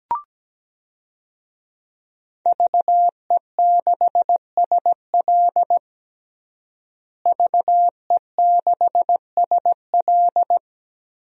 Morse code, heard as a keyed tone, about 17 wpm, 700 Hz. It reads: VE6SL VE6SL
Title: DXCC-100 Call Signs - 5 Characters - Repeated 2x - 17wpm